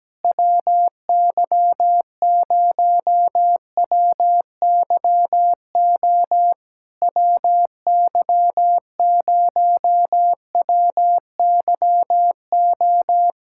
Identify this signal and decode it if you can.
Morse code, heard as a keyed tone, about 17 wpm, 700 Hz. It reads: WY0WYO WY0WYO
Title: SOTA QSO Practice 17wpm